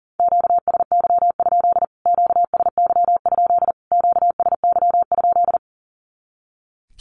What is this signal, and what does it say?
Morse code, heard as a keyed tone, about 40 wpm, 700 Hz. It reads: QSY? QSY? QSY?
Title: Q-Codes Repeated 3x 40wpm